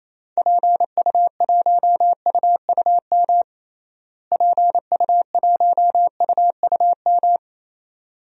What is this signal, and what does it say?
Morse code, heard as a keyed tone, about 27 wpm, 700 Hz. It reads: PU1UUM PU1UUM
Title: DXCC-100 Call Signs - 6 Characters - Repeated 2x - 28wpm